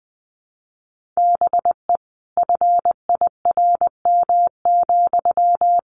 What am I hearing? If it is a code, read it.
Morse code, heard as a keyed tone, about 20 wpm, 700 Hz. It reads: BE FIRM,
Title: Sentences from Top 1400 Words 20wpm